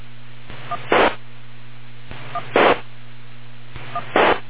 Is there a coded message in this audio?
Yes: DTMF from 0.7 to 4.0 s, heard as telephone touch tones, tones of about 60 ms. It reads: 1 1 1